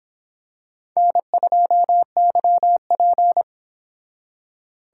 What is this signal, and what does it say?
Morse code, heard as a keyed tone, about 26 wpm, 700 Hz. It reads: N2YP